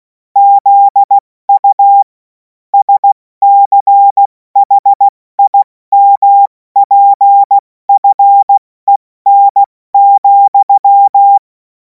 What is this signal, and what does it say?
Morse code, heard as a keyed tone, about 16 wpm, spaced as for 14 wpm, 800 Hz. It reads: ZU SCHIMPFEN,
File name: Morsekurs / Witze / Witz 03. Februar 2026 70 F80